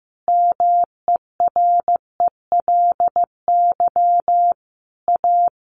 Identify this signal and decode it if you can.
Morse code, heard as a keyed tone, about 15 wpm, 700 Hz. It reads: MERELY A
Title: Sentences from Top 1400 Words 15wpm